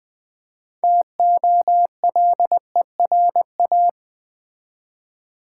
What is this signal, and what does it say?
Morse code, heard as a keyed tone, about 20 wpm, 700 Hz. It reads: TOLERA
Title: Build A Word - 3 Letters At A Time - 8-Letter Words - 20wpm